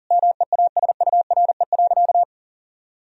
Morse code, heard as a keyed tone, about 40 wpm, 700 Hz. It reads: MEASURE.
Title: Sentences from Top 1000 Words - Review Part 2 - 40wpm